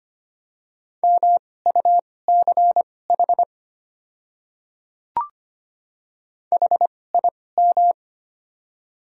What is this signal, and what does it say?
Morse code, heard as a keyed tone, about 25 wpm, spaced as for 18 wpm, 690 Hz. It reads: MUCH HIM
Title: 2x Character Spacing - Top 200 Words - 25wpm